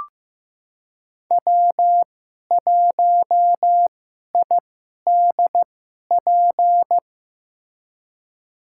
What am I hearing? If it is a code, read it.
Morse code, heard as a keyed tone, about 15 wpm, spaced as for 11 wpm, 700 Hz. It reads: W1IDP
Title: USA Call Signs - 2x Character Spacing - 15wpm